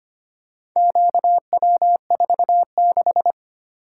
Morse code, heard as a keyed tone, about 25 wpm, 700 Hz. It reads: QW46
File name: practice / 25wpm / Four Letter-Number Combinations - 25wpm